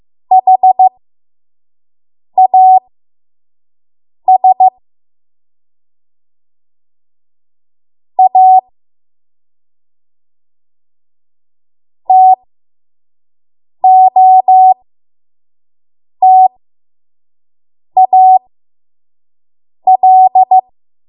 Morse code, heard as a keyed tone, about 15 wpm, spaced as for 5 wpm, 750 Hz. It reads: HAS A TOTAL
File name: Morse / 05WPM